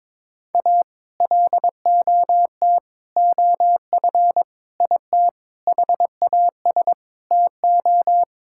Morse code, heard as a keyed tone, about 22 wpm, 700 Hz. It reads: A LOT OF IT HAS TO
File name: practice / 22wpm / Sets of 7 Words - 22wpm